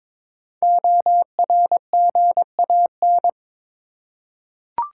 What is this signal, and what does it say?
Morse code, heard as a keyed tone, about 22 wpm, 690 Hz. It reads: ORGAN
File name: 5 Letter Words - First 3 Letters Given - 22wpm